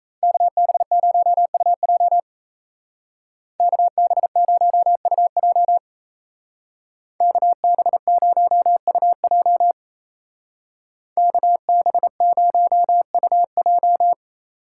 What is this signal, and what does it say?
Morse code, heard as a keyed tone, about 34 wpm, 700 Hz. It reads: KB0UJ KB0UJ KB0UJ KB0UJ
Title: Speed Racing - USA Call Signs 28wpm